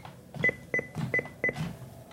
There is no code.